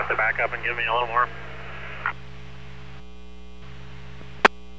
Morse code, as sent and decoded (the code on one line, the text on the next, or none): none